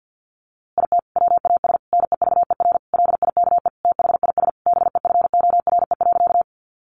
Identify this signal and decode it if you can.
Morse code, heard as a keyed tone, about 50 wpm, 700 Hz. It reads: IT WAS NEVER LIKE THIS BEFORE.
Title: A Sound Of Thunder 50wpm